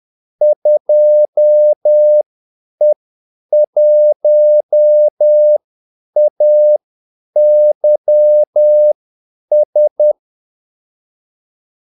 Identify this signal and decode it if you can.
Morse code, heard as a keyed tone, about 10 wpm, spaced as for 8 wpm, 600 Hz. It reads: 2E1AYS
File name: Callsigns-10-8wpm